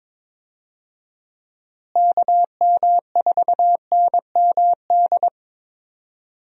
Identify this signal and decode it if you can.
Morse code, heard as a keyed tone, about 21 wpm, 700 Hz. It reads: KM4NMD
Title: USA Call Signs - 6 Characters - Repeated 2x - 22wpm